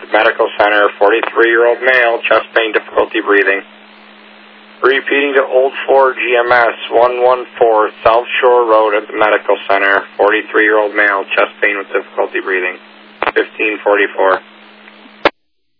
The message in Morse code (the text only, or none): none